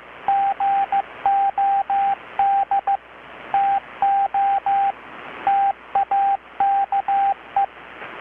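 Morse code, heard as a keyed tone, about 15 wpm, 780 Hz. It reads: GOD TO TAKE
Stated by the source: Morse from KPH's Night of Nights XX maritime radiotelegraphy event, received in Chiapas, Mexico about 3,800 km away on a Kenwood R-600 with a grounded 10 m long-wire antenna.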